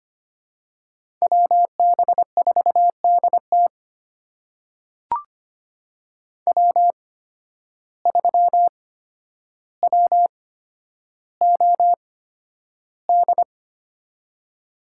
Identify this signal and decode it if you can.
Morse code, heard as a keyed tone, about 25 wpm, spaced as for 11 wpm, 700 Hz. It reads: WB4DT W3WOD
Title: USA Call Signs - 5 Characters - 8x Character Spacing - 25wpm